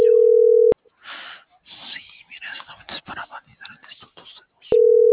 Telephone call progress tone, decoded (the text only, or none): ringback tone